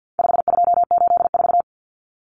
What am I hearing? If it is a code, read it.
Morse code, heard as a keyed tone, about 50 wpm, 700 Hz. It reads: 5284
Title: Common Cut Numbers - 4-Digit 50wpm